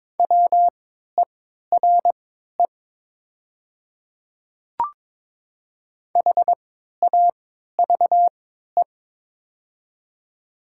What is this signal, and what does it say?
Morse code, heard as a keyed tone, about 22 wpm, spaced as for 13 wpm, 700 Hz. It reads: WERE HAVE
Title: 3x Character Spacing - Top 100 Words - 22wpm